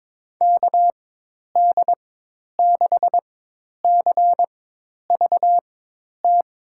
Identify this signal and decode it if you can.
Morse code, heard as a keyed tone, about 22 wpm, spaced as for 10 wpm, 700 Hz. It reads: KD6CVT
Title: USA Call Signs - 6 Characters - 4x Character Spacing - 22wpm